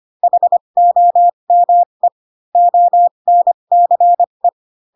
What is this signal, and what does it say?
Morse code, heard as a keyed tone, about 25 wpm, spaced as for 22 wpm, 700 Hz. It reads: HOME ONCE
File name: Pinocchio / Pinocchio ch8 18 25 ik1hge